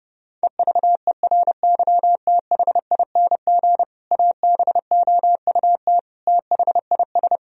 Morse code, heard as a keyed tone, about 32 wpm, 700 Hz. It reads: EVERYTHING ABOUT THIS
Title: Sentences from Top 1000 Words - Review Part 1 - 30wpm